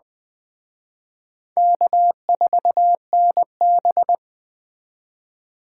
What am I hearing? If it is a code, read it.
Morse code, heard as a keyed tone, about 20 wpm, 700 Hz. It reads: K4NB